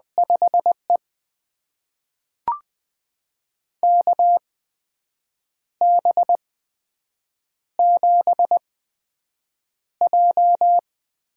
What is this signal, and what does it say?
Morse code, heard as a keyed tone, about 20 wpm, spaced as for 7 wpm, 700 Hz. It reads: 5E KB7J
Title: USA Call Signs - 4 Characters - 8x Character Spacing - 20wpm